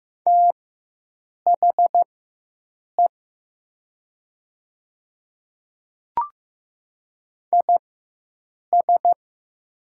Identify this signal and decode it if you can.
Morse code, heard as a keyed tone, about 15 wpm, spaced as for 7 wpm, 700 Hz. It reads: THE IS
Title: 4x Character Spacing - Top 200 Words - 15wpm